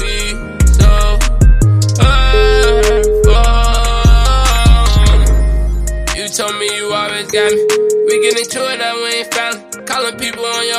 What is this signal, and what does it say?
Ringback tone (Europe, 425 Hz), with noise.